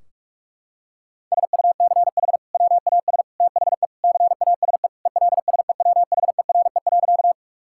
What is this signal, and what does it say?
Morse code, heard as a keyed tone, about 42 wpm, 700 Hz. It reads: SUCH WAS THE CASE ELSEWHERE.
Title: Sentences from Top 1600 Words 45wpm